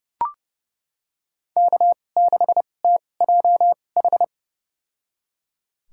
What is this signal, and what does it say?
Morse code, heard as a keyed tone, about 30 wpm, spaced as for 22 wpm, 700 Hz. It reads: K6TJH
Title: USA Call Signs - 5 Characters - 2x Character Spacing - 30wpm